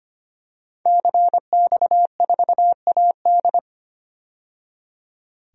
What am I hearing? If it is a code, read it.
Morse code, heard as a keyed tone, about 25 wpm, 700 Hz. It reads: CX4AD
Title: DXCC-100 Call Signs - 5 Characters - 25wpm